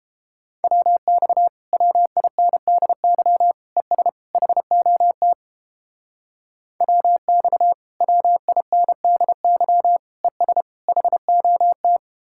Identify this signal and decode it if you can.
Morse code, heard as a keyed tone, about 31 wpm, 700 Hz. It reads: WX WINDY ES HOT WX WINDY ES HOT